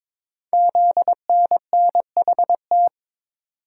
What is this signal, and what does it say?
Morse code, heard as a keyed tone, about 23 wpm, 700 Hz. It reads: ZNNHT
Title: Code Groups - 5 Letters - 22wpm